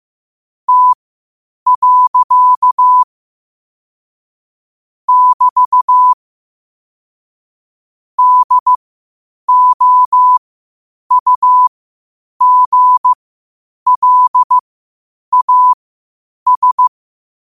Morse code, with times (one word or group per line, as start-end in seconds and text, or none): 0.7-3.0 T.
5.1-6.1 <BT>
8.2-16.9 DOUGLAS